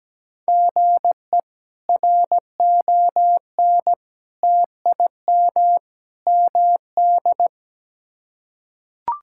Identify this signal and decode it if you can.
Morse code, heard as a keyed tone, about 17 wpm, 700 Hz. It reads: GE RON TIM MD
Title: K1USN SST QSO Practice - 17wpm